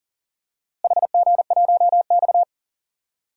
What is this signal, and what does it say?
Morse code, heard as a keyed tone, about 40 wpm, 700 Hz. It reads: HG1X
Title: DXCC-100 Call Signs - 4 Characters - Repeated 2x - 40wpm